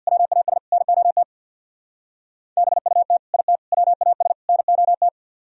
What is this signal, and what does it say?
Morse code, heard as a keyed tone, about 62 wpm, 700 Hz. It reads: WAS NOT BUT IT WAS NOT
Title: Speed-Racing - Sets of 4 Words 50wpm